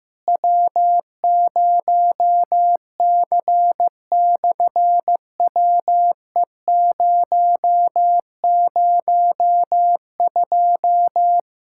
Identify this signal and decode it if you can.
Morse code, heard as a keyed tone, about 15 wpm, 700 Hz. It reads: W0C/WE002
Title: SOTA QSO Practice 15wpm